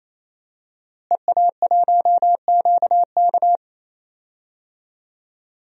Morse code, heard as a keyed tone, about 28 wpm, 700 Hz. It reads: EA1QK
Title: DXCC-100 Call Signs - 5 Characters - Repeated 3x - 28wpm